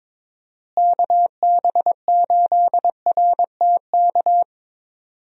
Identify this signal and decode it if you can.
Morse code, heard as a keyed tone, about 22 wpm, 700 Hz. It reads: KB8RTK
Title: USA Call Signs - 6 Characters - Repeated 2x - 22wpm